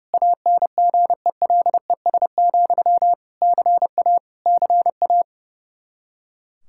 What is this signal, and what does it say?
Morse code, heard as a keyed tone, about 30 wpm, 700 Hz. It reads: ANGELES, CA CA
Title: Practice QSO - Exchange 1 - Beginning - 30wpm